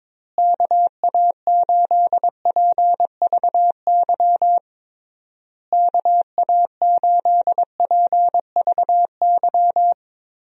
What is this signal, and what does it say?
Morse code, heard as a keyed tone, about 22 wpm, 700 Hz. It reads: KA8PVY KA8PVY
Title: USA Call Signs - 6 Characters - Repeated 3x - 22wpm